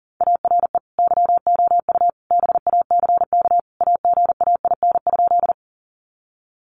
Morse code, heard as a keyed tone, about 40 wpm, 700 Hz. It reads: ARE YOU BACK AGAIN?
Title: Sentences from Top 200 Words 40wpm